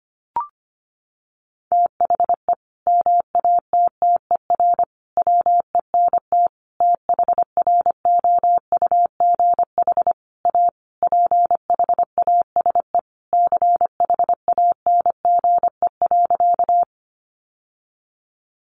Morse code, heard as a keyed tone, about 25 wpm, 700 Hz. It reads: THE MATTER WENT THROUGH A PHASE CHANGE.
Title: Sentences from Top 1200 Words 25wpm